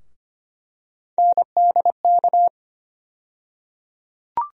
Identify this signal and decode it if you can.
Morse code, heard as a keyed tone, about 25 wpm, 700 Hz. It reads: NDK
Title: USA Call Sign Postfixes - 3 Letters 25wpm